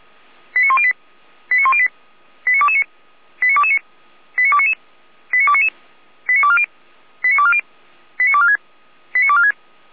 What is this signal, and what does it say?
Five-tone selective call (ZVEI), heard as quick runs of five tones, tones of about 70 ms. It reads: 89108 89108 89229 89229 89299 89299 89240 89240 89246 89246